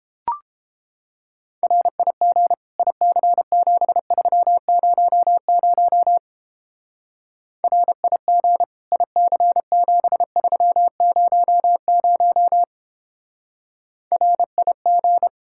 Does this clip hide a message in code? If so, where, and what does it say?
Yes: Morse code from 1.6 to 15.3 s, heard as a keyed tone, about 29 wpm, 700 Hz. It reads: RIG IC7300 RIG IC7300 RIG